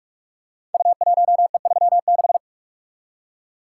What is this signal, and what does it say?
Morse code, heard as a keyed tone, about 44 wpm, 700 Hz. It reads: U1E36
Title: Code Groups - 5 Letter:Numbers - 45wpm